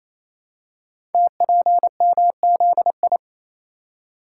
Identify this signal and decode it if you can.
Morse code, heard as a keyed tone, about 28 wpm, 700 Hz. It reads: TPMZI
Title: Code Groups - 5 Letters - 28wpm